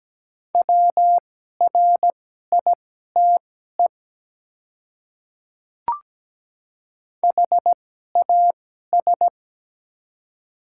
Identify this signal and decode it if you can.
Morse code, heard as a keyed tone, about 17 wpm, spaced as for 12 wpm, 700 Hz. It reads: WRITE HAS
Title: 2x Character Spacing - Top 300 Words - 17wpm